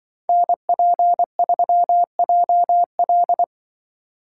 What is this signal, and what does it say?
Morse code, heard as a keyed tone, about 24 wpm, 700 Hz. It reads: NP3JL
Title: Speed Racing - USA 2x2 Call Signs 20wpm